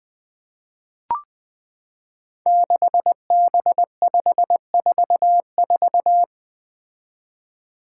Morse code, heard as a keyed tone, about 20 wpm, 700 Hz. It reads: 6B544
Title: Five Challenging Letter-Number Combinations 20wpm